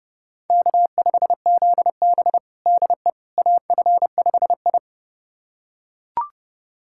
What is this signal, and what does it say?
Morse code, heard as a keyed tone, about 32 wpm, 700 Hz. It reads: K5ZB DE AF5I
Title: USA Callsign DE Callsign - 4 Characters - 30wpm